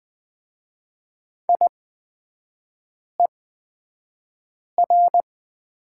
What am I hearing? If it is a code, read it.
Morse code, heard as a keyed tone, about 20 wpm, spaced as for 5 wpm, 700 Hz. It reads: IER
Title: 229 Lesson 37e - Letter x - Three Characters 20wpm